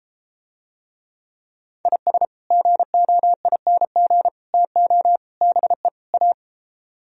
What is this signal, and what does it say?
Morse code, heard as a keyed tone, about 33 wpm, 700 Hz. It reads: IS GOING TO BE A